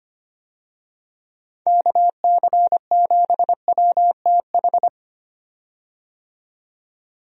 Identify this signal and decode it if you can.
Morse code, heard as a keyed tone, about 25 wpm, 700 Hz. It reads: KC7WTH
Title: USA Call Signs - 6 Characters - 25wpm